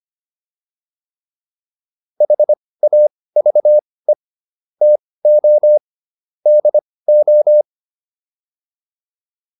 Morse code, heard as a keyed tone, about 25 wpm, spaced as for 18 wpm, 600 Hz. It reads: HAVE TO DO
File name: WD204 18